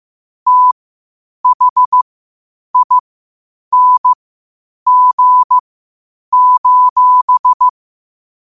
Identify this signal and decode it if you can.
Morse code, heard as a keyed tone, about 15 wpm, spaced as for 8 wpm, 1000 Hz. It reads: THING: